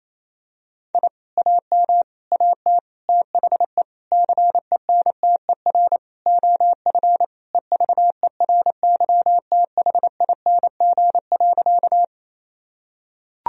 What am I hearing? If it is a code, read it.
Morse code, heard as a keyed tone, about 28 wpm, 700 Hz. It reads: I AM AT THE CENTER OF EVERYTHING.
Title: Sentences from Top 1000 Words - Review Part 2 - 28wpm